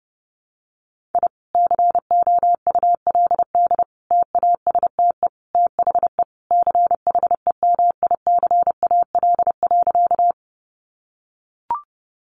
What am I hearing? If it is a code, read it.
Morse code, heard as a keyed tone, about 30 wpm, 700 Hz. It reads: I COULD TASTE THE CHEMICAL.
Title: Sentences from Top 1200 Words 30wpm